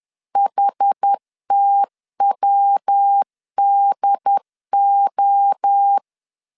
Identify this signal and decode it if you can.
Morse code, heard as a keyed tone, about 10 wpm, 800 Hz. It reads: HTWDO
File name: CW / DET / 1 3 1500C S B 10wpm 3335